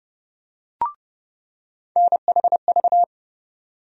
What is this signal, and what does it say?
Morse code, heard as a keyed tone, about 30 wpm, 700 Hz. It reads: NHV
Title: USA Call Sign Postfixes - 3 Letters 30wpm